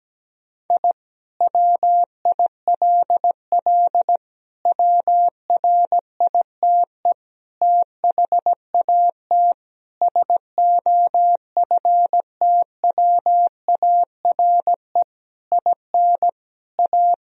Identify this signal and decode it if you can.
Morse code, heard as a keyed tone, about 17 wpm, 700 Hz. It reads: I WILL WRITE THAT SOFTWARE IN A
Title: Sentences from Top 1000 Words - Review Part 3 - 17wpm